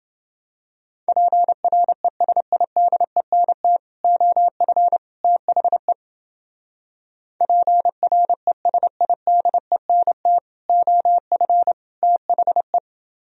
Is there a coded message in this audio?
Yes: Morse code from 1.1 to 12.8 s, heard as a keyed tone, about 28 wpm, 700 Hz. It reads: PRESIDENT OF THE PRESIDENT OF THE